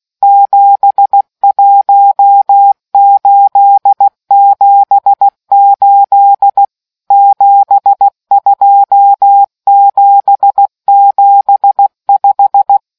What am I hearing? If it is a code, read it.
Morse code, heard as a keyed tone, about 16 wpm, 800 Hz. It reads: 71878 72775